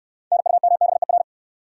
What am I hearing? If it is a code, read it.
Morse code, heard as a keyed tone, about 62 wpm, 700 Hz. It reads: NUMBER